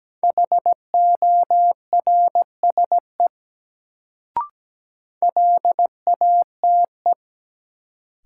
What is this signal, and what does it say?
Morse code, heard as a keyed tone, about 17 wpm, 700 Hz. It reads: HORSE LATE
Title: Sentences from Top 300 Words - New Word Drill - 17wpm